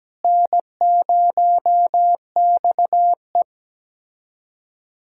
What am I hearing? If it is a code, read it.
Morse code, heard as a keyed tone, about 17 wpm, 700 Hz. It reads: N0XE
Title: USA Callsign DE Callsign - 4 Characters - 17wpm